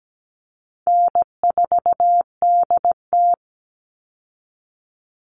Morse code, heard as a keyed tone, about 17 wpm, 700 Hz. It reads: N4DT
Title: Easy Call Signs - 17wpm